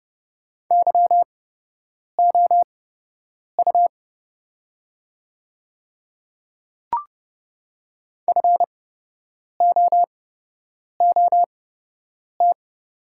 Morse code, heard as a keyed tone, about 30 wpm, spaced as for 8 wpm, 700 Hz. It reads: YOU FOOT
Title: 8x Character Spacing - Top 500 Words - 30wpm